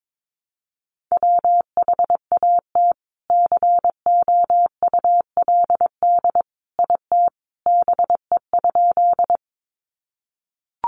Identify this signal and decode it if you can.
Morse code, heard as a keyed tone, about 22 wpm, 700 Hz. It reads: WHAT COULD IT BE?
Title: Sentences from Top 100 Words 22wpm